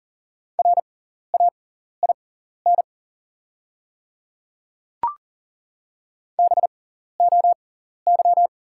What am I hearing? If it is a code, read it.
Morse code, heard as a keyed tone, about 40 wpm, spaced as for 14 wpm, 690 Hz. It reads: RAIN BOY
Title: 6x Character Spacing - Top 300 Words - 40wpm